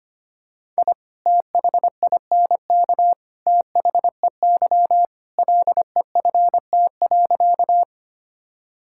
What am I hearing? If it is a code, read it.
Morse code, heard as a keyed tone, about 25 wpm, 700 Hz. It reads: I THINK THEY LEFT.
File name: practice / 25wpm / Sentences with 3-Word Phrases - 25wpm